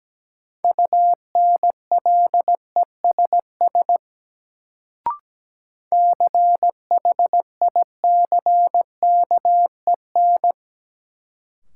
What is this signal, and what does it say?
Morse code, heard as a keyed tone, about 17 wpm, 700 Hz. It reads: UNLESS CHICKEN
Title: Sentences from Top 1000 Words - Review Part 5 - New Word Drill - 17wpm